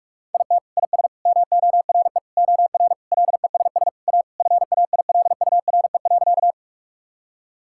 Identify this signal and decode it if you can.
Morse code, heard as a keyed tone, about 41 wpm, 700 Hz. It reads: IT IS MORE OR LESS A FAILURE.
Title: Sentences from Top 1000 Words - Review Part 6 - 45wpm